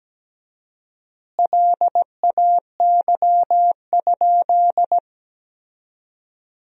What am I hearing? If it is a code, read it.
Morse code, heard as a keyed tone, about 17 wpm, 700 Hz. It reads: LAY?
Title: 4x Word-Spacing - Sentences from Top 400 Words - 17wpm